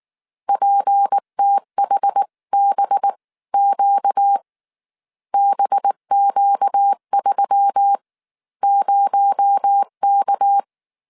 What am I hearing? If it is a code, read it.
Morse code, heard as a keyed tone, about 19 wpm, spaced as for 15 wpm, 800 Hz. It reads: PTHBQ BQ30K
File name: CW / DET / 1 6 1500C S C 18wpm 2356